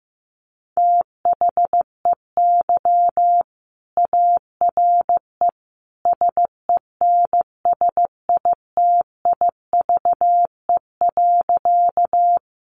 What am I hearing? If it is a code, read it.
Morse code, heard as a keyed tone, about 15 wpm, 700 Hz. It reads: THEY ARE SENSITIVE.
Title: Sentences from Top 1400 Words 15wpm